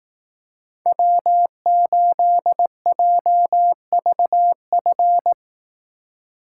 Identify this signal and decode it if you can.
Morse code, heard as a keyed tone, about 18 wpm, 700 Hz. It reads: W8JVF
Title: Speed Racing - USA 1x3 Call Signs 15wpm